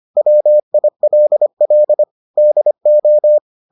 Morse code, heard as a keyed tone, about 25 wpm, 600 Hz. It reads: WILL DO